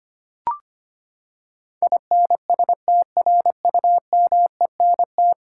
Morse code, heard as a keyed tone, about 25 wpm, 700 Hz. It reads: INSTRUMENT
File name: Ten Letter Words - 25wpm